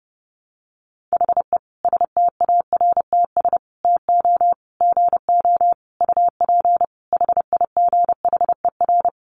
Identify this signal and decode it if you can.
Morse code, heard as a keyed tone, about 30 wpm, 700 Hz. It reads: HE STARTS TO GO UP HIGHER
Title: Additional Easy Phrases 30wpm